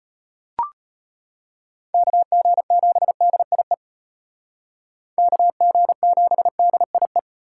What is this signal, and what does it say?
Morse code, heard as a keyed tone, about 33 wpm, 700 Hz. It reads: KG7DIE KG7DIE